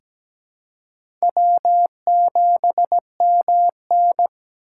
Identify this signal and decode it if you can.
Morse code, heard as a keyed tone, about 17 wpm, 700 Hz. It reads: W7MN
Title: USA Call Signs - 4 Characters - 4x Character Spacing - 17wpm